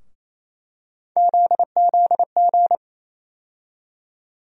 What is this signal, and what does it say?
Morse code, heard as a keyed tone, about 28 wpm, 700 Hz. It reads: ZZG
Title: USA Call Sign Postfixes - 3 Letters 28wpm